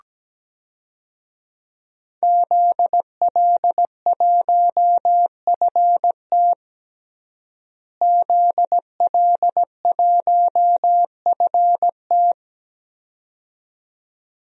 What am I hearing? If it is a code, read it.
Morse code, heard as a keyed tone, about 17 wpm, 700 Hz. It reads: ZL1FT ZL1FT
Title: DXCC-100 Call Signs - 5 Characters - Repeated 2x - 17wpm